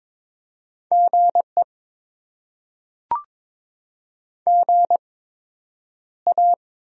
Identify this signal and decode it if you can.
Morse code, heard as a keyed tone, about 22 wpm, spaced as for 9 wpm, 700 Hz. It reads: GE GA
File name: Top 20 QSO Elements - 8x Character Spacing - 22wpm